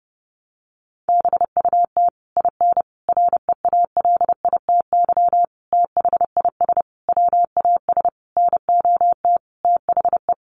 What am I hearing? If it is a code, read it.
Morse code, heard as a keyed tone, about 30 wpm, 700 Hz. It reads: BUT IN REALITY THIS WAS NOT THE
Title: Sentences from Top 1200 Words 30wpm